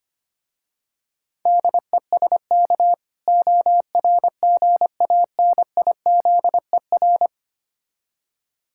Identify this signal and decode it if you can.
Morse code, heard as a keyed tone, about 25 wpm, 700 Hz. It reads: DESK ORGANIZER
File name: practice / 25wpm / Difficult Words with Hints - 25wpm